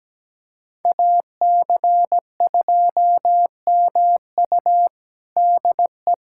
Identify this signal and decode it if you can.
Morse code, heard as a keyed tone, about 17 wpm, 700 Hz. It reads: AC2MU DE
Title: USA Callsign DE Callsign - 5 Characters - 17wpm